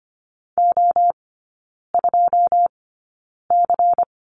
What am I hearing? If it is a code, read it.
Morse code, heard as a keyed tone, about 25 wpm, spaced as for 9 wpm, 700 Hz. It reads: O2C
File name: course / 25:10wpm / 093 Lesson 16e - Letter c - Three Characters 25wpm